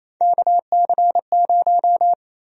Morse code, heard as a keyed tone, about 28 wpm, 700 Hz. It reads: KC0
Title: USA Call Sign Prefixes - 28wpm